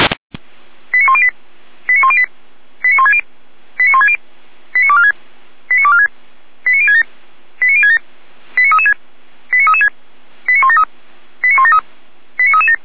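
Five-tone selective call (ZVEI), heard as quick runs of five tones, tones of about 70 ms. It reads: 89108 89108 89160 89160 89246 89246 89067 89067 89336 89336 89173 89173 89337
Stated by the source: ZVEI five-tone selective call received on an SDR.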